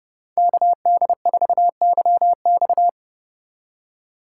Morse code, heard as a keyed tone, about 29 wpm, 700 Hz. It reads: KD4YX